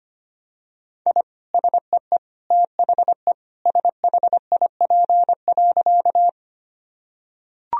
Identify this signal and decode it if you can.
Morse code, heard as a keyed tone, about 25 wpm, 700 Hz. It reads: I SEE THE SHIP.
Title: Sentences from Top 1000 Words - Review Part 2 - 25wpm